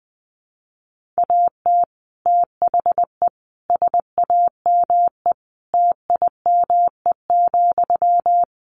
Morse code, heard as a keyed tone, about 20 wpm, 700 Hz. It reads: AT THE SAME TIME,
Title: Sentences from Top 1500 Words 20wpm